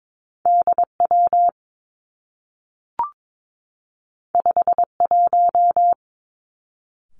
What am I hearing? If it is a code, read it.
Morse code, heard as a keyed tone, about 22 wpm, 700 Hz. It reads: D W 5 1
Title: Two Challenging Letter-Number Combinations 22wpm